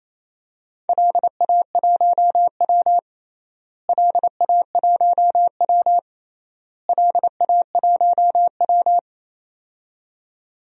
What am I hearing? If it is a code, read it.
Morse code, heard as a keyed tone, about 28 wpm, 700 Hz. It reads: LA1W LA1W LA1W
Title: DXCC-100 Call Signs - 4 Characters - Repeated 3x - 28wpm